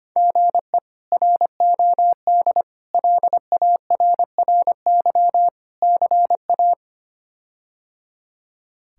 Morse code, heard as a keyed tone, about 25 wpm, 700 Hz. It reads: GE ROD LARRY CA
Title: K1USN SST QSO Practice - 25wpm